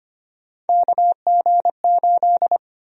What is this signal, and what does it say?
Morse code, heard as a keyed tone, about 25 wpm, 700 Hz. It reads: KG8
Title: USA Call Sign Prefixes - 25wpm